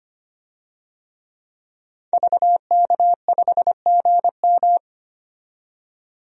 Morse code, heard as a keyed tone, about 25 wpm, 700 Hz. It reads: VK5GM
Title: DXCC-100 Call Signs - 5 Characters - 25wpm